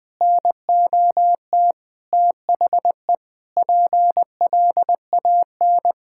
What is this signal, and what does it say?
Morse code, heard as a keyed tone, about 20 wpm, 700 Hz. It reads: NOT THE PLAN
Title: Sentences with 8-Word Phrases - 20wpm